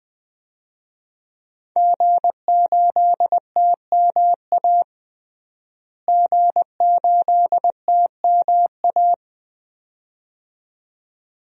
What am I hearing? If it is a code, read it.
Morse code, heard as a keyed tone, about 20 wpm, 700 Hz. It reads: G8TMA G8TMA
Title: DXCC-100 Call Signs - 5 Characters - Repeated 2x - 20wpm